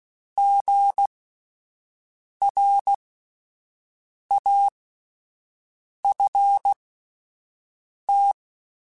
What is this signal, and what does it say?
Morse code, heard as a keyed tone, about 16 wpm, spaced as for 5 wpm, 800 Hz. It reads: GRAFT